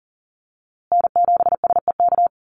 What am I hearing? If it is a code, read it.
Morse code, heard as a keyed tone, about 40 wpm, 700 Hz. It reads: N7SEK
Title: USA Call Signs Repeated 2x 40wpm